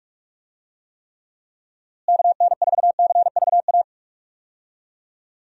Morse code, heard as a keyed tone, about 43 wpm, 700 Hz. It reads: KN4CVA
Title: USA Call Signs - 6 Characters - Repeated 2x - 45wpm